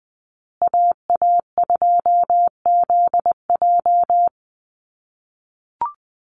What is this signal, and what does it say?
Morse code, heard as a keyed tone, about 20 wpm, 700 Hz. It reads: AA2ZJ
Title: USA Call Signs Repeated 2x 20wpm